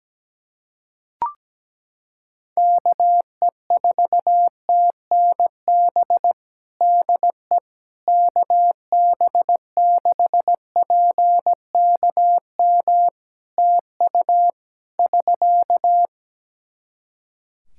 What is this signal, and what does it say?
Morse code, heard as a keyed tone, about 17 wpm, 700 Hz. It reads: KE4TNB DE KB6PKM TU <SK>
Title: Practice QSO - Exchange 4 - Ending - 17wpm